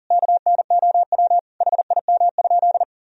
Morse code, heard as a keyed tone, about 40 wpm, 700 Hz. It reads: KNOW HIM?
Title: Sentences from Top 100 Words and Abbreviations 40wpm